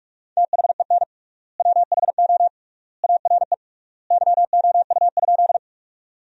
Morse code, heard as a keyed tone, about 46 wpm, 690 Hz. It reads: THEN WHO ARE YOU?
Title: 3x Word-Spacing - Sentences from Top 100 Words - 45wpm